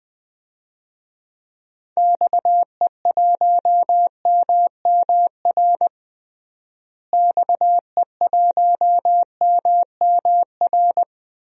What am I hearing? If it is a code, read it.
Morse code, heard as a keyed tone, about 20 wpm, 700 Hz. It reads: XE1MMR XE1MMR
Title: DXCC-100 Call Signs - 6 Characters - Repeated 3x - 20wpm